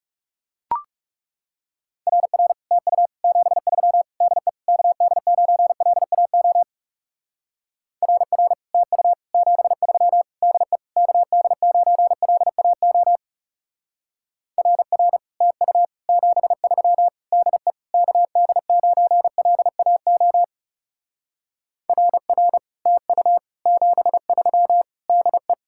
Morse code, heard as a keyed tone, about 36 wpm, 700 Hz. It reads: RR TU 73 DE KD9LAO RR TU 73 DE KD9LAO RR TU 73 DE KD9LAO RR TU 73 DE
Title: Speed-Racing Common QSO Phrases Encore - 30wpm